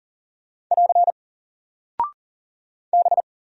Morse code, heard as a keyed tone, about 40 wpm, 700 Hz. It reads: <AR> B
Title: Single Character (Letter:Number:Punctuation:Prosign) Alt 40wpm